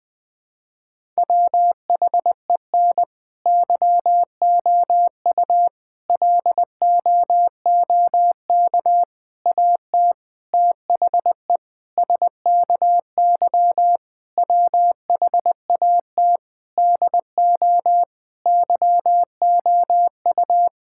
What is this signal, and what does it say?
Morse code, heard as a keyed tone, about 20 wpm, 700 Hz. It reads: WHEN YOU LOOK AT THE SKY WHAT DO YOU
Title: Sentences with 5-Word Phrases - Spoken Beginning - 20wpm